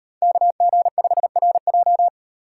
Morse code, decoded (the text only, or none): KG5RJ